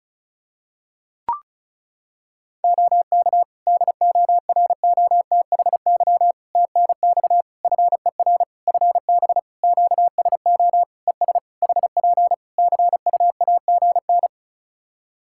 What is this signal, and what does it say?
Morse code, heard as a keyed tone, about 34 wpm, 700 Hz. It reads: OK DOROTHY TNX FER FB QSO ES HP CUAGN
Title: Practice QSO - Exchange 4 - Ending - 35wpm